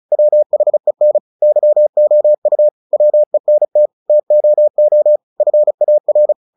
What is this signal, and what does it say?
Morse code, heard as a keyed tone, about 34 wpm, 600 Hz. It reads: WHEN YOU WENT TOO FAR
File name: the old man and the sea 2 35wpm